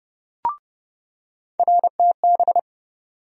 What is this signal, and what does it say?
Morse code, heard as a keyed tone, about 30 wpm, 700 Hz. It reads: RTB
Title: USA Call Sign Postfixes - 3 Letters 30wpm